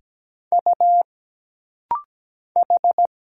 Morse code, heard as a keyed tone, about 17 wpm, 700 Hz. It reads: U H
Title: Single Letters - Rapid-Fire 17wpm